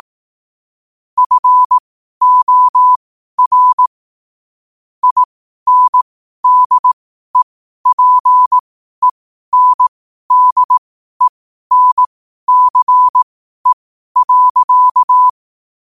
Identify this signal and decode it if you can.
Morse code, heard as a keyed tone, about 18 wpm, spaced as for 12 wpm, 1000 Hz. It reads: FOR INDEPENDENCE.
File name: qotdfiles / 12wpm